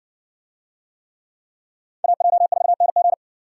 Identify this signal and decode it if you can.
Morse code, heard as a keyed tone, about 60 wpm, 700 Hz. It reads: AJ4NG